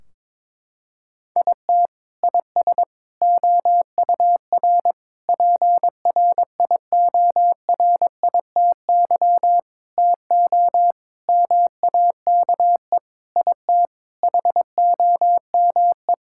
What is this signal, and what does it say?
Morse code, heard as a keyed tone, about 22 wpm, 700 Hz. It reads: IT IS OUR PRIORITY TO MAKE IT HOME
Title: Sentences from Top 1600 Words 22wpm